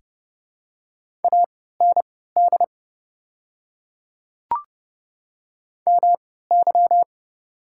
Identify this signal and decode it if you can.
Morse code, heard as a keyed tone, about 30 wpm, spaced as for 17 wpm, 700 Hz. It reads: AND MY
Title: 3x Character Spacing - Top 100 Words - 30wpm